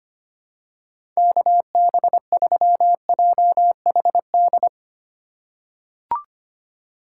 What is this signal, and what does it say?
Morse code, heard as a keyed tone, about 25 wpm, 700 Hz. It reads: KB3JHD
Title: USA Call Signs - 6 Characters - 2x Character Spacing - 25wpm